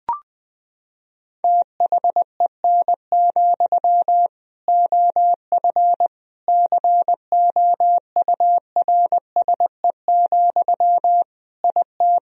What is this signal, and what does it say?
Morse code, heard as a keyed tone, about 20 wpm, 700 Hz. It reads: THEN, OF COURSE, IT
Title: Sentences from Top 1000 Words - Review Part 1 - 20wpm